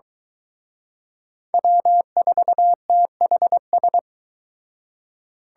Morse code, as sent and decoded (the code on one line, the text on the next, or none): .-- ....- - .... ...
W4THS